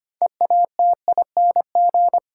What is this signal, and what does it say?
Morse code, heard as a keyed tone, about 25 wpm, 700 Hz. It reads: EATING
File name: practice / 25wpm / Difficult Sentences with Hints - 25wpm